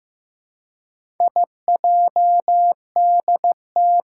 Morse code, heard as a keyed tone, about 15 wpm, 700 Hz. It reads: IJDT
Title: Four Challenging Letter-Number Combinations 15wpm